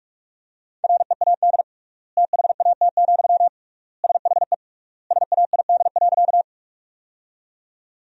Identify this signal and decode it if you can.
Morse code, heard as a keyed tone, about 47 wpm, 700 Hz. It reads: READ THAT, SHE SAID.
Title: 3x Word-Spacing - Sentences from Top 200 Words - 45wpm